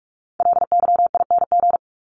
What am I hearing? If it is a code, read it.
Morse code, heard as a keyed tone, about 44 wpm, 700 Hz. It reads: LYING
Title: ING Suffix - Top 100 Words with Base Words 45wpm